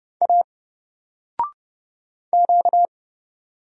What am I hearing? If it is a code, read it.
Morse code, heard as a keyed tone, about 30 wpm, 700 Hz. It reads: A Q
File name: Single-Letter-Number-30wpm